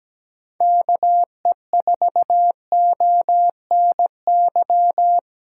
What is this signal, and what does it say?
Morse code, heard as a keyed tone, about 21 wpm, 700 Hz. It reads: KE4ONY